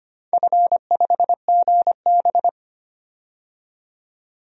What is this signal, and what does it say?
Morse code, heard as a keyed tone, about 25 wpm, 700 Hz. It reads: F5GB
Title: DXCC-100 Call Signs - 4 Characters - 25wpm